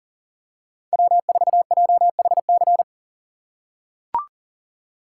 Morse code, heard as a keyed tone, about 37 wpm, 700 Hz. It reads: W4JHC
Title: USA Call Signs - 5 Characters - 2x Character Spacing - 40wpm